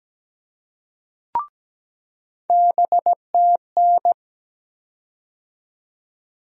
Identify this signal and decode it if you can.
Morse code, heard as a keyed tone, about 17 wpm, 700 Hz. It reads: BTN
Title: USA Call Sign Postfixes - 3 Letters 17wpm